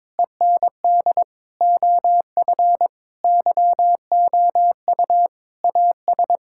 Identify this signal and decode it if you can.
Morse code, heard as a keyed tone, about 22 wpm, 700 Hz. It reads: END OF YOU AS
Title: Sentences from Top 1000 Words - Review Part 6 - 22wpm